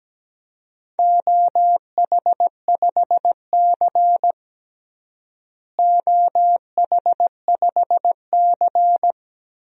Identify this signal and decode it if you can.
Morse code, heard as a keyed tone, about 17 wpm, 700 Hz. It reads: OH5C OH5C
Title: DXCC-100 Call Signs - 4 Characters - Repeated 3x - 17wpm